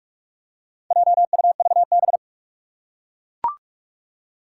Morse code, heard as a keyed tone, about 43 wpm, 700 Hz. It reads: JUVB
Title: Four Challenging Letter-Number Combinations 45wpm